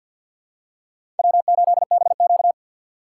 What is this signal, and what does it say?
Morse code, heard as a keyed tone, about 47 wpm, 700 Hz. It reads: W8BQ
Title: USA Call Signs - 4 Characters - 2x Character Spacing - 50wpm